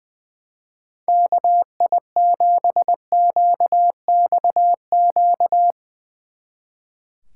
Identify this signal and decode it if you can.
Morse code, heard as a keyed tone, about 20 wpm, 700 Hz. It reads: KI7QXQ
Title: Speed Racing - USA 2x3 Call Signs 20wpm